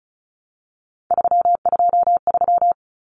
Morse code, heard as a keyed tone, about 34 wpm, 700 Hz. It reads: 323
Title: Common Cut Numbers - 3-Digit 35wpm